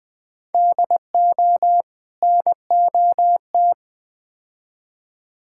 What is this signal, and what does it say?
Morse code, heard as a keyed tone, about 20 wpm, 700 Hz. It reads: DO NOT
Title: Build A Phrase - 4-Word Phrases - 20wpm